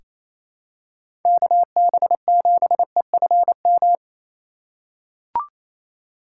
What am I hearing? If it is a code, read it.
Morse code, heard as a keyed tone, about 28 wpm, 700 Hz. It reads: KB7EFM
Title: USA Call Signs - 6 Characters - 4x Character Spacing - 28wpm